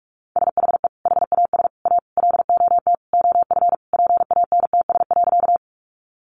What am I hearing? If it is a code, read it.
Morse code, heard as a keyed tone, about 44 wpm, 700 Hz. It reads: SHE HAS A LOT OF PANTS.
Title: Additional Easy Phrases 45wpm